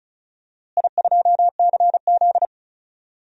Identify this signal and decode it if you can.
Morse code, heard as a keyed tone, about 34 wpm, 700 Hz. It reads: I2CZ
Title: DXCC-100 Call Signs - 4 Characters - Repeated 3x - 35wpm